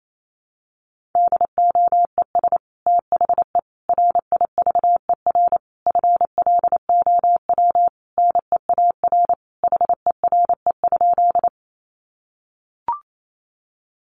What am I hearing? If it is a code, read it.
Morse code, heard as a keyed tone, about 28 wpm, 700 Hz. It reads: DOES THE RIVER FLOW NEAR HERE?
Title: Sentences from Top 1200 Words 28wpm